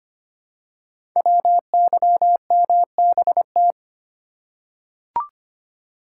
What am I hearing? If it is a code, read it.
Morse code, heard as a keyed tone, about 25 wpm, 700 Hz. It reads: WYMBT
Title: Code Groups - 5 Letters - 25wpm